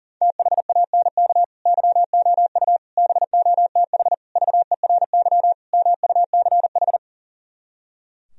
Morse code, heard as a keyed tone, about 50 wpm, 700 Hz. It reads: THANK YOU BOTH VERY MUCH
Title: Speed-Racing - Sets of 5 Words 40wpm